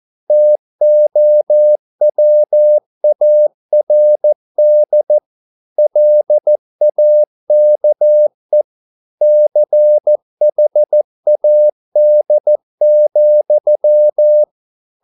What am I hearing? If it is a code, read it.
Morse code, heard as a keyed tone, about 14 wpm, 600 Hz. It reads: TOWARD LAKE CHAD,